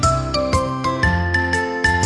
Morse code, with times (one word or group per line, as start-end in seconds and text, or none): none